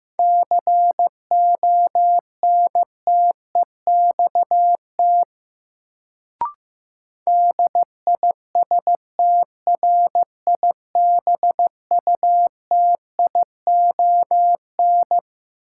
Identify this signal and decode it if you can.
Morse code, heard as a keyed tone, about 15 wpm, 700 Hz. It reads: CONTEXT DISTRIBUTION
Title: Sentences from Top 1000 Words - Review Part 7 - New Word Drill - 15wpm